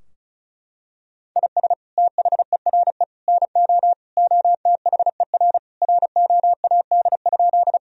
Morse code, heard as a keyed tone, about 36 wpm, 700 Hz. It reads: IS THERE NO OTHER ROAD?
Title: Sentences from Top 1000 Words - Review Part 2 - 35wpm